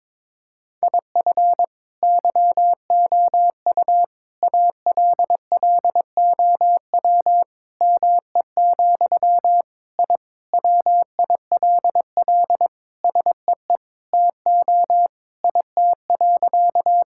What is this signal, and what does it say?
Morse code, heard as a keyed tone, about 22 wpm, 700 Hz. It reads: IF YOU ALLOW ME, I WILL SEE TO IT.
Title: Sentences from Top 1000 Words - Review Part 5 - 22wpm